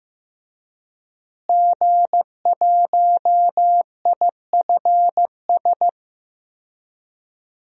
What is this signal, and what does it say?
Morse code, heard as a keyed tone, about 15 wpm, 700 Hz. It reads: G1IFS
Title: DXCC-100 Call Signs - 5 Characters - Repeated 3x - 15wpm